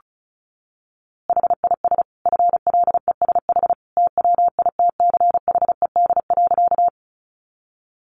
Morse code, heard as a keyed tone, about 34 wpm, 700 Hz. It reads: HIS FLESH TWITCHED.
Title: A Sound Of Thunder 35wpm